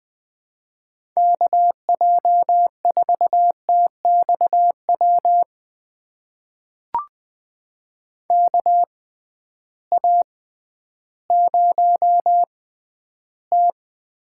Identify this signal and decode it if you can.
Morse code, heard as a keyed tone, about 20 wpm, spaced as for 12 wpm, 700 Hz. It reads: KJ4TXW KA0T
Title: USA Call Signs - 6x Character Spacing - 20wpm